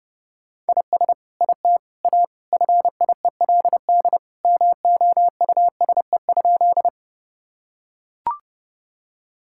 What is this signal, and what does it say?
Morse code, heard as a keyed tone, about 30 wpm, 700 Hz. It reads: IS IT A FIELD MOUSE?
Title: Sentences from Top 1000 Words - Review Part 5 - 30wpm